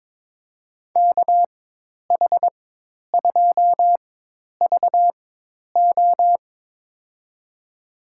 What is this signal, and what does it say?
Morse code, heard as a keyed tone, about 22 wpm, spaced as for 10 wpm, 700 Hz. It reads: KH2VO
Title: USA Call Signs - 4x Character Spacing - 22wpm